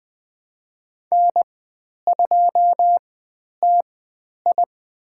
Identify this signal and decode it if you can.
Morse code, heard as a keyed tone, about 20 wpm, spaced as for 10 wpm, 700 Hz. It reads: N2TI
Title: 109 Lesson 18g - Letter m - Call Signs 20wpm